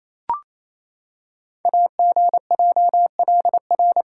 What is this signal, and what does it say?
Morse code, heard as a keyed tone, about 28 wpm, 700 Hz. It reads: AGJLR
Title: Code Groups - 5 Letters - 28wpm